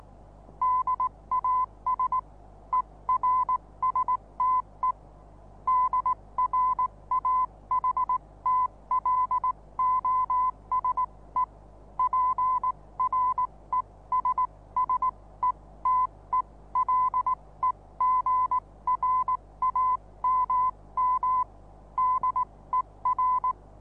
Morse code, heard as a keyed tone, about 19 wpm, spaced as for 16 wpm, 1010 Hz. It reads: DAS ERSTE DRAHTLOSE PRESSETELEGRAMM DER
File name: repos / Morsekurs / DJ4UF Morsen26